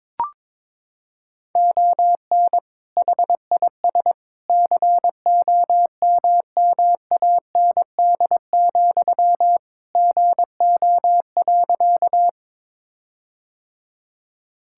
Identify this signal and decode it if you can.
Morse code, heard as a keyed tone, about 22 wpm, 700 Hz. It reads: ON HIS COMMAND, GO.
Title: Sentences from Top 1600 Words 22wpm